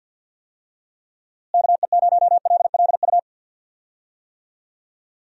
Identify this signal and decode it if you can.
Morse code, heard as a keyed tone, about 51 wpm, 700 Hz. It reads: KE0LLU
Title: USA Call Signs - 6 Characters - Repeated 3x - 50wpm